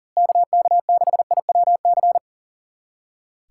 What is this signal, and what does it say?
Morse code, heard as a keyed tone, about 40 wpm, 700 Hz. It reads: KK6IWC
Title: USA Callsign DE Callsign - 6 Characters - 40wpm